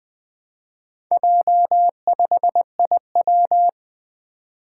Morse code, heard as a keyed tone, about 20 wpm, 700 Hz. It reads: J5IW
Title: Four Letter-Number Combinations - 20wpm